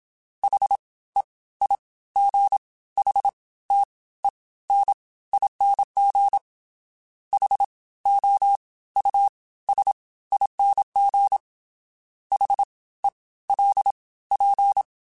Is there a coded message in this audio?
Yes: Morse code from 0.4 to 14.8 s, heard as a keyed tone, about 27 wpm, spaced as for 17 wpm, 800 Hz. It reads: HEIGHTENING HOUSING HELP